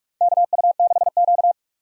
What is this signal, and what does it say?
Morse code, heard as a keyed tone, about 45 wpm, 700 Hz. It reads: KU6Q
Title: USA Call Signs - 4 Characters - 45wpm